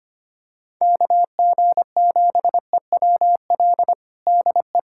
Morse code, heard as a keyed tone, about 25 wpm, 700 Hz. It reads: KG7EWL DE
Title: USA Callsign DE Callsign - 6 Characters - 25wpm